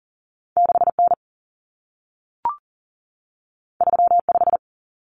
Morse code, heard as a keyed tone, about 40 wpm, 700 Hz. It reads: 6N 35
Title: Common Cut Numbers - 2-Digit 40wpm